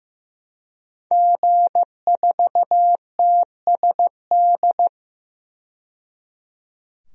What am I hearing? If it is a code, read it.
Morse code, heard as a keyed tone, about 15 wpm, 700 Hz. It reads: G4TSD
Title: DXCC-100 Call Signs - 5 Characters - 15wpm